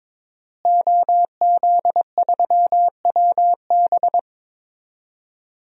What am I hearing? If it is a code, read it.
Morse code, heard as a keyed tone, about 22 wpm, 700 Hz. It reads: OZ3WB
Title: DXCC-100 Call Signs - 5 Characters - Repeated 3x - 22wpm